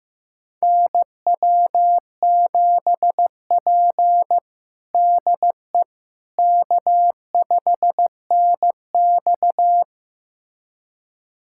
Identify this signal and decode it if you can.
Morse code, heard as a keyed tone, about 15 wpm, 700 Hz. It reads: NW7P DE K5NX
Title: USA Callsign DE Callsign - 4 Characters - 15wpm